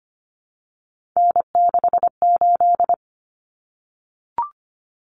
Morse code, heard as a keyed tone, about 25 wpm, 700 Hz. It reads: N68
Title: Common Cut Numbers - 3-Digit 25wpm